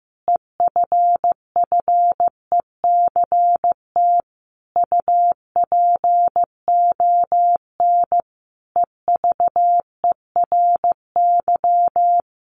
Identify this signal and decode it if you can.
Morse code, heard as a keyed tone, about 15 wpm, 700 Hz. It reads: EFFECT UPON EVERY
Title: Sentences from Top 1100 Words 15wpm